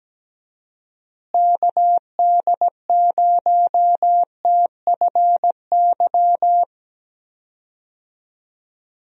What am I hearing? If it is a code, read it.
Morse code, heard as a keyed tone, about 17 wpm, 700 Hz. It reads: KD0TFY
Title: USA Call Signs - 6 Characters - Repeated 2x - 17wpm